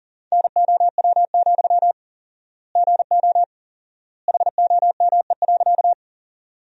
Morse code, heard as a keyed tone, about 40 wpm, 700 Hz. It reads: NOW, GO HOME.
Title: 4x Word-Spacing - Sentences from Top 200 Words - 40wpm